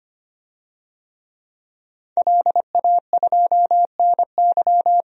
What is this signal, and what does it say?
Morse code, heard as a keyed tone, about 25 wpm, 700 Hz. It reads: LA2NY
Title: DXCC-100 Call Signs - 5 Characters - 25wpm